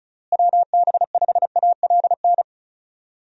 Morse code, heard as a keyed tone, about 34 wpm, 700 Hz. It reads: WB5ALN
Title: USA Call Signs - 6 Characters - Repeated 2x - 35wpm